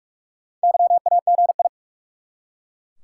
Morse code, heard as a keyed tone, about 44 wpm, 700 Hz. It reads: YAGI
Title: Top 180 QSO Elements 45wpm